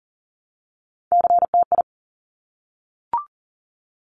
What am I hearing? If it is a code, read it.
Morse code, heard as a keyed tone, about 42 wpm, 700 Hz. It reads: CTI
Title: Top Three Letter Combinations in Words 40wpm